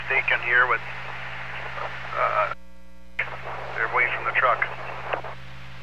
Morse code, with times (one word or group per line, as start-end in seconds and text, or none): none